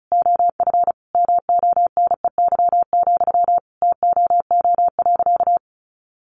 Morse code, heard as a keyed tone, about 34 wpm, 700 Hz. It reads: OF MONEY, TOO.
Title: Sentences from Top 900 Words 35wpm